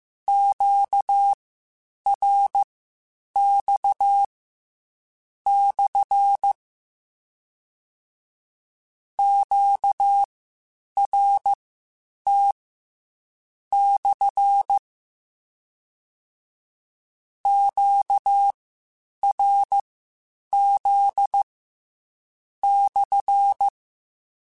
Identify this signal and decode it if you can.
Morse code, heard as a keyed tone, about 15 wpm, spaced as for 7 wpm, 800 Hz. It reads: QRX/ QRT/ QRZ/
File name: Q Calls / Q Calls-mp3s / Q CALLS SPSL1 @13